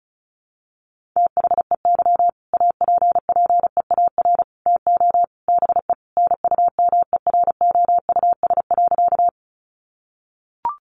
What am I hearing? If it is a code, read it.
Morse code, heard as a keyed tone, about 34 wpm, 700 Hz. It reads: THEY APPEAR TO BE NUMEROUS.
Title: Sentences from Top 1400 Words 35wpm